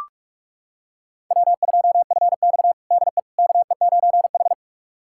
Morse code, heard as a keyed tone, about 44 wpm, 690 Hz. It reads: W2FX DE KE9H
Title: USA Callsign DE Callsign - 4 Characters - 45wpm